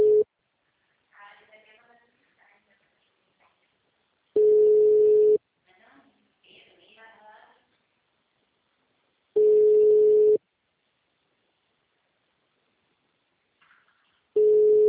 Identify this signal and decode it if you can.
Ringback tone (Europe, 425 Hz), with speech.
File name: out- -2004- - - .22